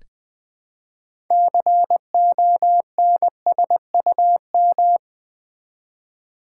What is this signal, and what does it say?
Morse code, heard as a keyed tone, about 20 wpm, 700 Hz. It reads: CONSUM
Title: Build A Word - 3 Letters At A Time - 8-Letter Words - 20wpm